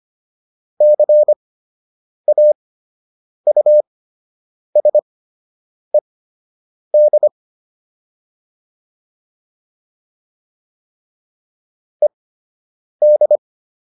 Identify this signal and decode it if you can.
Morse code, heard as a keyed tone, about 25 wpm, spaced as for 8 wpm, 600 Hz. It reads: CAUSED ED
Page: fundamental-practice-files